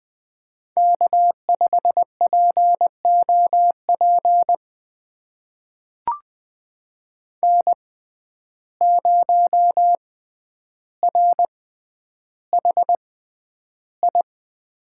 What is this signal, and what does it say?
Morse code, heard as a keyed tone, about 20 wpm, spaced as for 10 wpm, 700 Hz. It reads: K5POP N0RHI
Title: USA Call Signs - 5 Characters - 6x Character Spacing - 20wpm